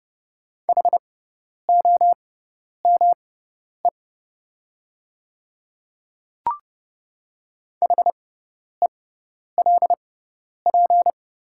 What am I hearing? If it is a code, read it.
Morse code, heard as a keyed tone, about 31 wpm, spaced as for 10 wpm, 700 Hz. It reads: HOME HELP
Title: 6x Character Spacing - Top 200 Words - 30wpm